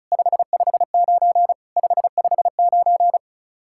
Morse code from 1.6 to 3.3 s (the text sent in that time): 559